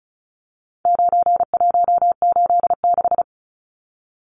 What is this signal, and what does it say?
Morse code, heard as a keyed tone, about 34 wpm, 700 Hz. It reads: 9186
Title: Numbers - 4-Digit 35wpm